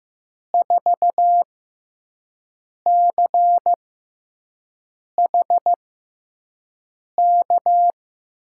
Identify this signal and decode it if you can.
Morse code, heard as a keyed tone, about 15 wpm, spaced as for 5 wpm, 700 Hz. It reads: 4CH K
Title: USA Call Signs - 6x Character Spacing - 15wpm